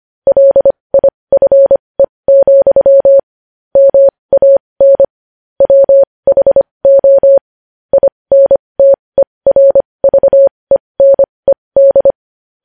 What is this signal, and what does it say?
Morse code, heard as a keyed tone, about 25 wpm, spaced as for 20 wpm, 550 Hz. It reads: LIFE, MAN WHO INTERVENED
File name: podcast / News headlines-1x-25x20